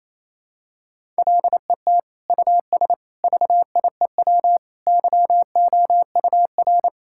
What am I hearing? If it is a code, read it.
Morse code, heard as a keyed tone, about 28 wpm, 700 Hz. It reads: LET US VIEW YOUR
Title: Sentences from Top 1000 Words - Review Part 6 - 28wpm